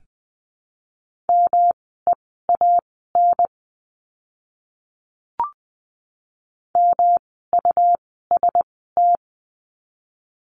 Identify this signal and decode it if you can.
Morse code, heard as a keyed tone, about 20 wpm, spaced as for 14 wpm, 700 Hz. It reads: MEAN MUST
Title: 237 Lesson 38f - Letter q - Words 20wpm